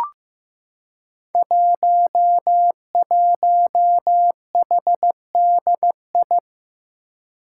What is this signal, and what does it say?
Morse code, heard as a keyed tone, about 15 wpm, 700 Hz. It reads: 11HDI
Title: Five Challenging Letter-Number Combinations 15wpm